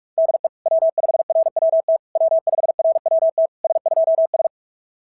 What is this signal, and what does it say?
Morse code, heard as a keyed tone, about 44 wpm, 690 Hz. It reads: DE W5RWT W5RWT S2S
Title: SOTA QSO Practice 45wpm